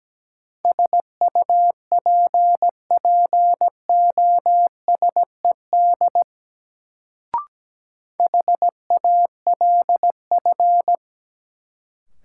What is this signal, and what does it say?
Morse code, heard as a keyed tone, about 17 wpm, 700 Hz. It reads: SUPPOSED HALF
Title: Sets of 3 Words - New Word Drill - 17wpm